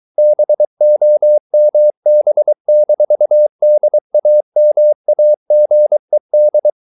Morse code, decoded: BOMB-DAMAGED

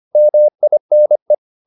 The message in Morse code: MINE

